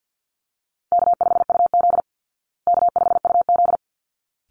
Morse code, heard as a keyed tone, about 48 wpm, 700 Hz. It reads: K5UZ K5UZ
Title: Easy Call Signs Repeated 2x - 50wpm